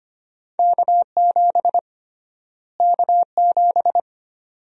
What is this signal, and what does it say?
Morse code, heard as a keyed tone, about 25 wpm, 700 Hz. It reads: K7 K7
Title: USA Call Sign Prefixes - Repeated 2x - 25wpm